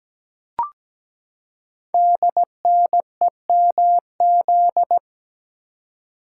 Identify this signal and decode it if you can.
Morse code, heard as a keyed tone, about 17 wpm, 700 Hz. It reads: DNEMZ
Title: Code Groups - 5 Letters - 17wpm